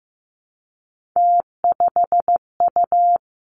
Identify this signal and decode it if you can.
Morse code, heard as a keyed tone, about 15 wpm, 700 Hz. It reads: T5U
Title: Three Challenging Letter-Number Combinations 15wpm